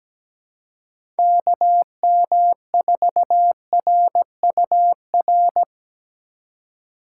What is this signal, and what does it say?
Morse code, heard as a keyed tone, about 17 wpm, 700 Hz. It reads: KM4RUR
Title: USA Call Signs - 6 Characters - 4x Character Spacing - 17wpm